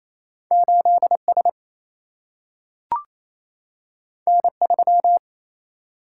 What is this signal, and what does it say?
Morse code, heard as a keyed tone, about 28 wpm, 700 Hz. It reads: 8S N3
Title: Single-Letter Single-Number 28wpm